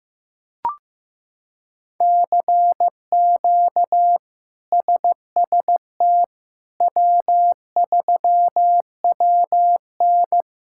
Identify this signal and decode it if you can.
Morse code, heard as a keyed tone, about 15 wpm, 700 Hz. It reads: CQ SST W3WN
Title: K1USN SST QSO Practice - 15wpm